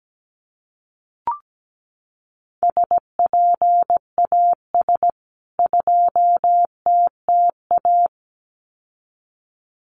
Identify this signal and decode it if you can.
Morse code, heard as a keyed tone, about 17 wpm, 700 Hz. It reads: SPAS 2TTA
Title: CWT Exchange 17wpm